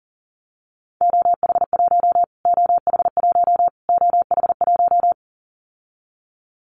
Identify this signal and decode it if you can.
Morse code, heard as a keyed tone, about 40 wpm, 700 Hz. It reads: OH1 OH1 OH1
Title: Callsign Prefixes for Top 25 DXCC - Repeated 3x - 40wpm